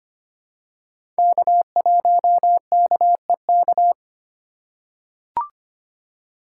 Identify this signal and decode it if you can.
Morse code, heard as a keyed tone, about 25 wpm, 700 Hz. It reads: K1KEK
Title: USA Call Signs - 5 Characters - 3x Character Spacing - 25wpm